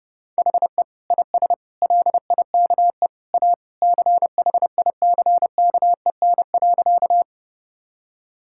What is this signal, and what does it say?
Morse code, heard as a keyed tone, about 32 wpm, 700 Hz. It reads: HE IS LIKE A CHICKEN.
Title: Sentences from Top 1000 Words - Review Part 5 - 30wpm